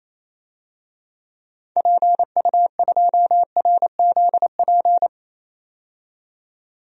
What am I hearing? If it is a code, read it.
Morse code, heard as a keyed tone, about 28 wpm, 700 Hz. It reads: PU2RZP
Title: DXCC-100 Call Signs - 6 Characters - 28wpm